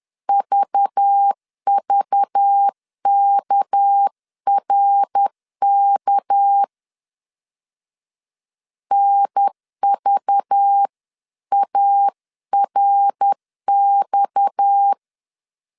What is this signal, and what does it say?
Morse code, heard as a keyed tone, about 10 wpm, 800 Hz. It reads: VVKRK NVARX
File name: CW / DET / 1 6 1500C S B 10wpm 4137